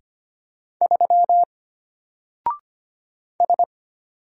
Morse code, heard as a keyed tone, about 25 wpm, 700 Hz. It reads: 3 S
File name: Single-Letter-Number-25wpm